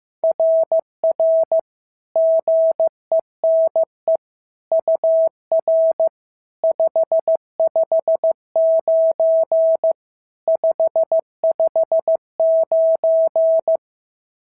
Morse code, heard as a keyed tone, about 14 wpm, 650 Hz. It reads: RR GENE UR 559 559
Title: SOTA QSO Practice 15wpm